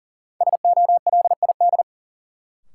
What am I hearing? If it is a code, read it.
Morse code, heard as a keyed tone, about 40 wpm, 700 Hz. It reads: SOLID
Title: Sentences from Top 1500 Words - New Word Drill - 40wpm